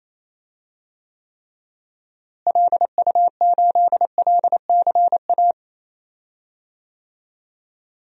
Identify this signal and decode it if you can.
Morse code, heard as a keyed tone, about 28 wpm, 700 Hz. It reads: LU8LCA
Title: DXCC-100 Call Signs - 6 Characters - Repeated 2x - 28wpm